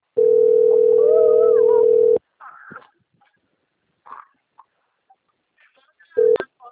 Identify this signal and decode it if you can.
Ringback tone (North America), 440 + 480 Hz.
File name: out- - - - - .15988